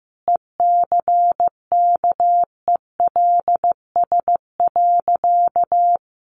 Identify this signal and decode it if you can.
Morse code, heard as a keyed tone, about 15 wpm, 700 Hz. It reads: ECKELS.
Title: A Sound Of Thunder 15wpm